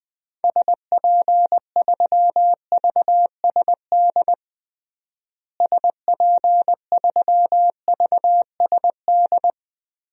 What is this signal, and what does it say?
Morse code, heard as a keyed tone, about 20 wpm, 700 Hz. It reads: SP3VSD SP3VSD
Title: DXCC-100 Call Signs - 6 Characters - Repeated 3x - 20wpm